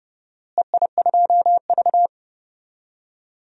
Morse code, heard as a keyed tone, about 30 wpm, 700 Hz. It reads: EI2V
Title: DXCC-100 Call Signs - 4 Characters - Repeated 2x - 30wpm